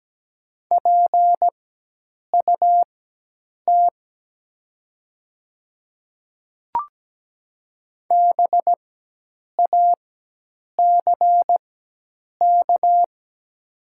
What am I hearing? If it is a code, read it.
Morse code, heard as a keyed tone, about 17 wpm, spaced as for 8 wpm, 700 Hz. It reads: PUT BACK
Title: 4x Character Spacing - Top 500 Words - 17wpm